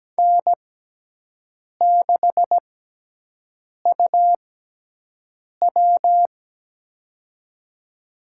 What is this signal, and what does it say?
Morse code, heard as a keyed tone, about 17 wpm, spaced as for 6 wpm, 700 Hz. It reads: N6UW
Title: USA Call Signs - 6x Character Spacing - 17wpm